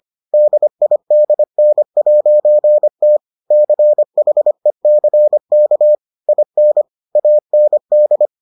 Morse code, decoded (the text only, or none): DIDN'T CHECK IN AND